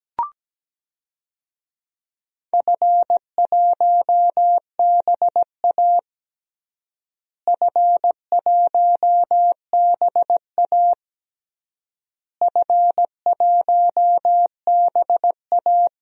Morse code, heard as a keyed tone, about 17 wpm, 700 Hz. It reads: F1BA F1BA F1BA
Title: DXCC-100 Call Signs - 4 Characters - Repeated 3x - 17wpm